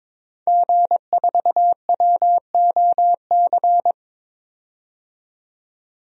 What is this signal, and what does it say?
Morse code, heard as a keyed tone, about 22 wpm, 700 Hz. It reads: G4WOC
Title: DXCC-100 Call Signs - 5 Characters - 22wpm